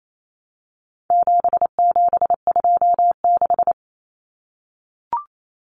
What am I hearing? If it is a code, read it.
Morse code, heard as a keyed tone, about 28 wpm, 700 Hz. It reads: 7726
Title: Numbers - 4-Digit 28wpm